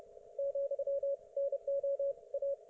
Morse code, heard as a keyed tone, about 29 wpm, 560 Hz, with radio noise. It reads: , NO A